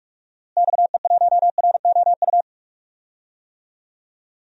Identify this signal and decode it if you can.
Morse code, heard as a keyed tone, about 43 wpm, 700 Hz. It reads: XE1ROU
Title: DXCC-100 Call Signs - 6 Characters - Repeated 3x - 45wpm